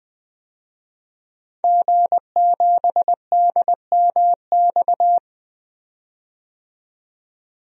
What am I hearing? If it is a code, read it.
Morse code, heard as a keyed tone, about 20 wpm, 700 Hz. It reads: G7DMX
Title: DXCC-100 Call Signs - 5 Characters - 20wpm